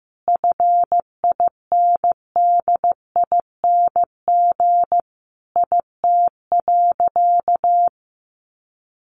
Morse code, heard as a keyed tone, about 15 wpm, 700 Hz. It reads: FINDING IT.
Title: Sentences from Top 1100 Words 15wpm